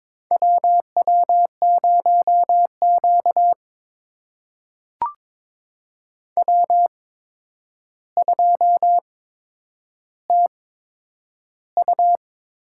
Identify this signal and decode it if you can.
Morse code, heard as a keyed tone, about 22 wpm, spaced as for 9 wpm, 700 Hz. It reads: WW0Q W2TU
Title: USA Call Signs - 4 Characters - 8x Character Spacing - 22wpm